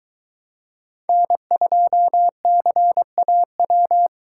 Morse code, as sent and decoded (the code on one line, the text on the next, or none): -. ..--- -.-. .- .--
N2CAW